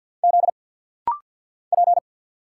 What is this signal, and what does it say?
Morse code, heard as a keyed tone, about 50 wpm, 700 Hz. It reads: Z P
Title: Single Letter-Number - Rapid-Fire 50wpm